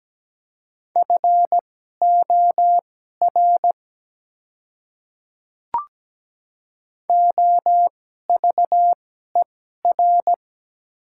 Morse code, heard as a keyed tone, about 17 wpm, spaced as for 12 wpm, 700 Hz. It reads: FOR OVER
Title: 2x Character Spacing - Top 100 Words - 17wpm